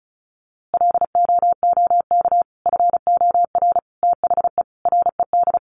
Morse code, heard as a keyed tone, about 34 wpm, 700 Hz. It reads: LOOK FOR THE RED
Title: A Sound Of Thunder 35wpm